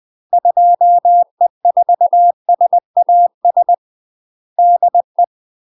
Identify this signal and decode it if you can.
Morse code, heard as a keyed tone, about 20 wpm, 700 Hz. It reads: 2E4SAS DE
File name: MASTER QSOs TRAINING/20WPM